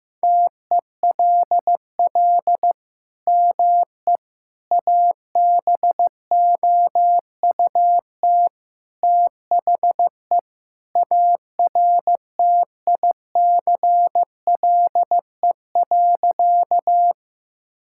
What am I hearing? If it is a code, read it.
Morse code, heard as a keyed tone, about 15 wpm, 700 Hz. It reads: TELL ME ABOUT THE ARTICLE.
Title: Sentences from Top 1000 Words - Review Part 5 - 15wpm